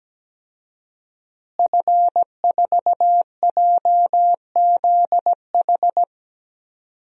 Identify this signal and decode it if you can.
Morse code, heard as a keyed tone, about 17 wpm, 700 Hz. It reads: F4JZH
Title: DXCC-100 Call Signs - 5 Characters - Repeated 2x - 17wpm